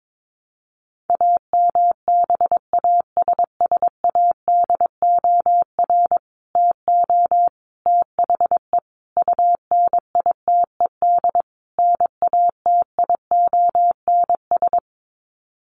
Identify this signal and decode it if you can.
Morse code, heard as a keyed tone, about 22 wpm, 700 Hz. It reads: AMBASSADOR TO THE UNITED NATIONS
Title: Sets of 5 Words Encore2 22wpm